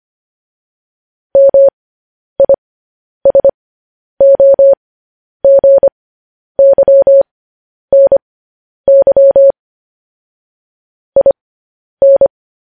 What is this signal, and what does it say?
Morse code, heard as a keyed tone, about 25 wpm, spaced as for 10 wpm, 550 Hz. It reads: MISOGYNY IN
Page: news headlines in morse code at 10 wpm